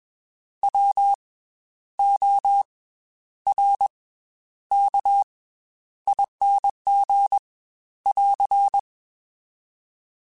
Morse code, heard as a keyed tone, about 21 wpm, 800 Hz. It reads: W O R K ING <AR>
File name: Special INGs / Special INGs-mp3s / Slow ing @20